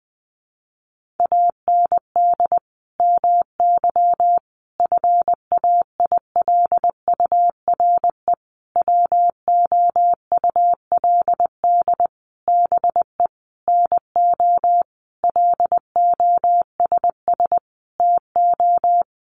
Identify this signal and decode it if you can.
Morse code, heard as a keyed tone, about 20 wpm, 700 Hz. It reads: AND MY FAILURE WOULD BE NO LOSS TO
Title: Sentences from Top 1400 Words 20wpm